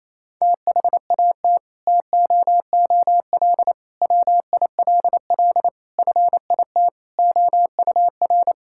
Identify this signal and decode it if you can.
Morse code, heard as a keyed tone, about 28 wpm, 700 Hz. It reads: THAT TOOL WILL FIT OUR
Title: Sentences from Top 1000 Words - Review Part 6 - 28wpm